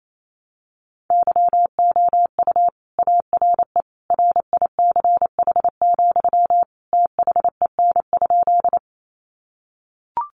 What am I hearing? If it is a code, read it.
Morse code, heard as a keyed tone, about 28 wpm, 700 Hz. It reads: YOU ARE RICH, THEN?
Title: Sentences from Top 1100 Words 28wpm